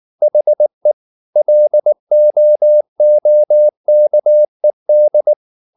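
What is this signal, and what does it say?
Morse code, heard as a keyed tone, about 19 wpm, 600 Hz. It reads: HE LOOKED
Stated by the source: Morse code at 19 wpm from CW med Gnister, reading Ken Follett's novel Never in English.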